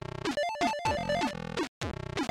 There is no code.